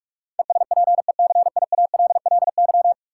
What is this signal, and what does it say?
Morse code, heard as a keyed tone, about 45 wpm, 700 Hz. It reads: ESPECIALLY
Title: Sentences from Top 700 Words - New Word Drill - 45wpm